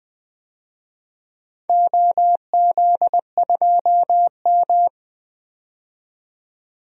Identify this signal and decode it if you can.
Morse code, heard as a keyed tone, about 21 wpm, 700 Hz. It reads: OZ2M
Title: DXCC-100 Call Signs - 4 Characters - 20wpm